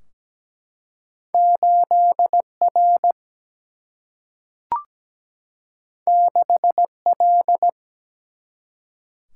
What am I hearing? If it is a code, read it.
Morse code, heard as a keyed tone, about 17 wpm, 700 Hz. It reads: 8R 6L
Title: Single-Letter Single-Number 17wpm